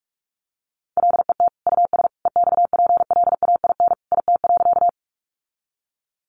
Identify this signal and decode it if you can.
Morse code, heard as a keyed tone, about 44 wpm, 700 Hz. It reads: LET US EXPLAIN IT.
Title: Sentences from Top 1100 Words 45wpm